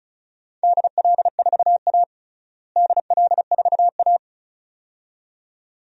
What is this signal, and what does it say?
Morse code, heard as a keyed tone, about 36 wpm, 700 Hz. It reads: DL4A DL4A
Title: DXCC-100 Call Signs - 4 Characters - Repeated 2x - 35wpm